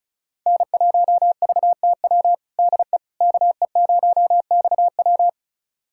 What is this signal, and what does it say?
Morse code, heard as a keyed tone, about 36 wpm, 700 Hz. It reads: N1VTW DE KE0XW
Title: USA Callsign DE Callsign - 5 Characters - 35wpm